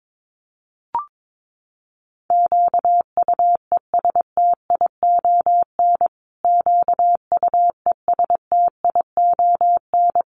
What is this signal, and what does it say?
Morse code, heard as a keyed tone, about 22 wpm, 700 Hz. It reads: QUESTION QUESTION
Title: Top 500 Words Repeated 22wpm